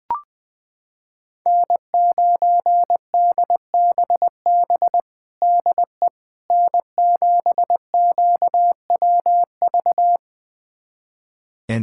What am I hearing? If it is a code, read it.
Morse code, heard as a keyed tone, about 20 wpm, 700 Hz. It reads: N9DBB DE N7QWV
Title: USA Callsign DE Callsign - 5 Characters - 20wpm